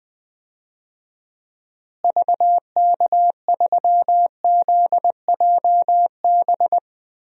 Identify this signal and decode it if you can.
Morse code, heard as a keyed tone, about 20 wpm, 700 Hz. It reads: VK3ZJB
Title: DXCC-100 Call Signs - 6 Characters - Repeated 2x - 20wpm